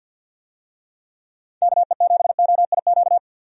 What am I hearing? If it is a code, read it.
Morse code, heard as a keyed tone, about 50 wpm, 700 Hz. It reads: KE7OIQ
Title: USA Call Signs - 6 Characters - 50wpm